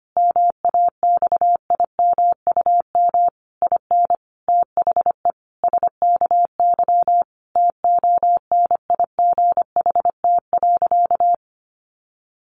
Morse code, heard as a keyed tone, about 25 wpm, 700 Hz. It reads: MAXIMUM IN THE SKY TONIGHT.
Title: Sentences from Top 1300 Words 25wpm